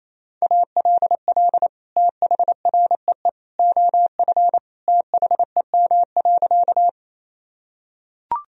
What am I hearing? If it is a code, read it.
Morse code, heard as a keyed tone, about 28 wpm, 700 Hz. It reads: ALL THREE OF THEM.
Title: Sentences from Top 1000 Words - Review Part 1 - 28wpm